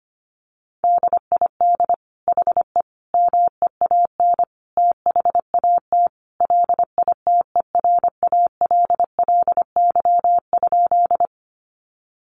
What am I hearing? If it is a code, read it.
Morse code, heard as a keyed tone, about 25 wpm, 700 Hz. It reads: DID HE MEAN THAT LITERALLY?
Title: Sentences from Top 1400 Words 25wpm